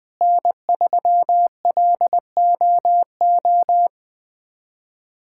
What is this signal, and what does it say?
Morse code, heard as a keyed tone, about 20 wpm, 700 Hz. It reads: N3LOO
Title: USA Callsign DE Callsign - 5 Characters - 20wpm